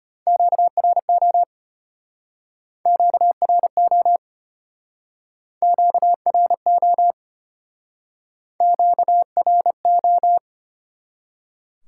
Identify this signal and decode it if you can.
Morse code, heard as a keyed tone, about 32 wpm, 700 Hz. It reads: QRO QRO QRO QRO